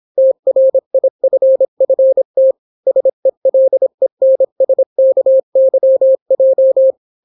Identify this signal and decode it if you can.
Morse code, heard as a keyed tone, about 26 wpm, 530 Hz. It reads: TRIFFT SELENSKYJ